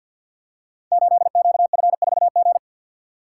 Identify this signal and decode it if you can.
Morse code, heard as a keyed tone, about 49 wpm, 700 Hz. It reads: 8QF4G
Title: Code Groups - 5 Letter:Numbers - 50wpm